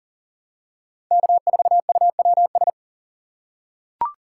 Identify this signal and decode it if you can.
Morse code, heard as a keyed tone, about 40 wpm, 700 Hz. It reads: K4UWS
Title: USA Call Signs - 5 Characters - 3x Character Spacing - 40wpm